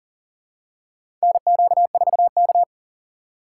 Morse code, heard as a keyed tone, about 38 wpm, 700 Hz. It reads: NQ4K
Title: USA Call Signs - 4 Characters - 8x Character Spacing - 40wpm